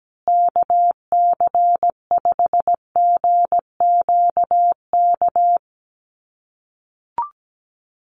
Morse code, heard as a keyed tone, about 17 wpm, 700 Hz. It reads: KC5GQK
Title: USA Call Signs Repeated 2x 17wpm